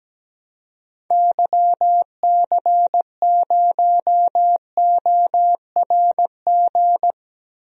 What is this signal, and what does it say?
Morse code, heard as a keyed tone, about 17 wpm, 700 Hz. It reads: YC0ORG
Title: DXCC-100 Call Signs - 6 Characters - Repeated 2x - 17wpm